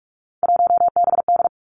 Morse code, heard as a keyed tone, about 46 wpm, 700 Hz. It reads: 1BD
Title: Three Challenging Letter-Number Combinations 45wpm